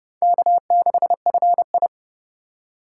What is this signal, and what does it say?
Morse code, heard as a keyed tone, about 25 wpm, 700 Hz. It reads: K6FI